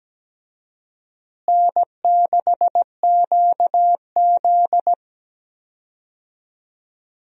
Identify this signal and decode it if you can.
Morse code, heard as a keyed tone, about 17 wpm, 700 Hz. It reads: N6QZ
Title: USA Call Signs - 4 Characters - Repeated 2x - 17wpm